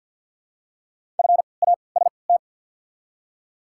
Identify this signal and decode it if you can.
Morse code, heard as a keyed tone, about 48 wpm, spaced as for 28 wpm, 700 Hz. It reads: FAST
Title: 3x Character Spacing - Top 500 Words - 50wpm